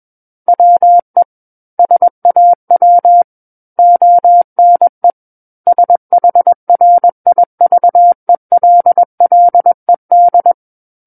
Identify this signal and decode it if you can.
Morse code, heard as a keyed tone, about 21 wpm, 700 Hz. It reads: WE SAW ONE SHRIVELLED
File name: Book One / 12-Chapter 12 - 21 WPM